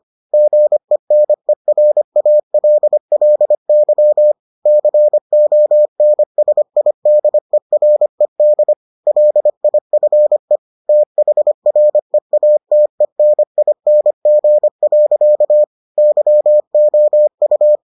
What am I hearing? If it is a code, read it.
Morse code, heard as a keyed tone, about 25 wpm, 600 Hz. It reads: GENERALLY CONSIDERED LIFE THREATENING. YOU